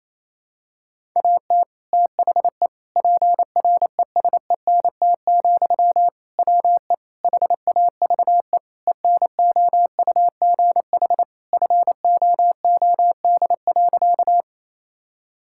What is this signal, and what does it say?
Morse code, heard as a keyed tone, about 28 wpm, 700 Hz. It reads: AT THE PRESENT, WE HAVE ENOUGH FOOD.
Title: Sentences from Top 1000 Words - Review Part 5 - 28wpm